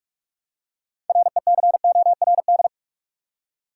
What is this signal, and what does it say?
Morse code, heard as a keyed tone, about 46 wpm, 700 Hz. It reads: RECORD
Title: Six Letter Words - 45wpm